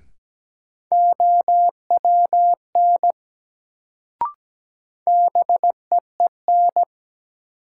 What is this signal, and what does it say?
Morse code, heard as a keyed tone, about 17 wpm, 700 Hz. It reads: OWN BEEN
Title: Sets of 2 Words - New Word Drill - 17wpm